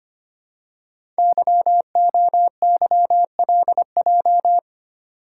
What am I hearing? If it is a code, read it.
Morse code, heard as a keyed tone, about 25 wpm, 700 Hz. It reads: YOYLJ
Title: Code Groups - 5 Letter:Numbers - 25wpm